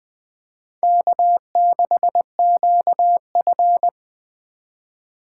Morse code, heard as a keyed tone, about 25 wpm, 700 Hz. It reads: K6QF
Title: Speed Racing - USA 1x2 Call Signs 20wpm